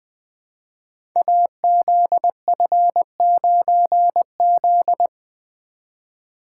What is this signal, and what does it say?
Morse code, heard as a keyed tone, about 20 wpm, 700 Hz. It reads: AZF9Z
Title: Code Groups - 5 Letter:Numbers - 20wpm